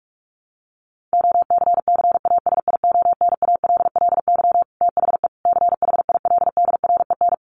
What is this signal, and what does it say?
Morse code, heard as a keyed tone, about 44 wpm, 700 Hz. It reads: OCCASIONALLY THE CHILDREN
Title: Sentences from Top 1300 Words 45wpm